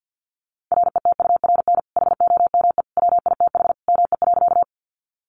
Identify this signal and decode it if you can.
Morse code, heard as a keyed tone, about 50 wpm, 700 Hz. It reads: RETURN HOME WITH ME.
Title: Sentences from Top 800 Words 50wpm